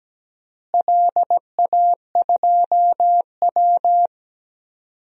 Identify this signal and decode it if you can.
Morse code, heard as a keyed tone, about 17 wpm, 700 Hz. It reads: LA2W
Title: DXCC-100 Call Signs - 4 Characters - Repeated 3x - 17wpm